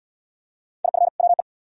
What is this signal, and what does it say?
Morse code, heard as a keyed tone, about 76 wpm, 700 Hz. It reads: IF WE